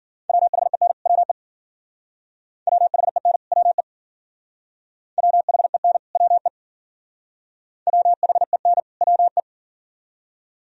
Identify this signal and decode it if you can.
Morse code, heard as a keyed tone, about 52 wpm, 700 Hz. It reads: WHEN WE WHEN WE WHEN WE WHEN WE